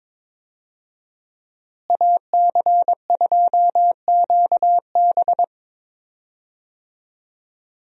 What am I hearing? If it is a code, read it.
Morse code, heard as a keyed tone, about 22 wpm, 700 Hz. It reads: AC2QB
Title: USA Call Signs - 5 Characters - Repeated 2x - 22wpm